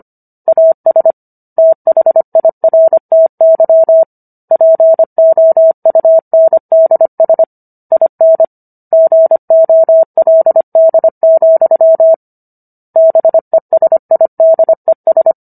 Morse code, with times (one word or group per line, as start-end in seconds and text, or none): 0.5-1.1 AS
1.6-4.0 THIRTY
4.5-7.4 POUNDS
7.9-8.4 IN
8.9-12.1 GOLD,
13.0-15.3 BESIDES